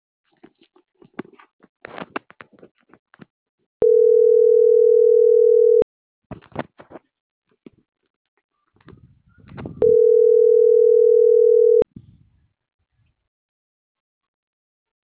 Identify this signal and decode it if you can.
Ringback tone (North America), 440 + 480 Hz.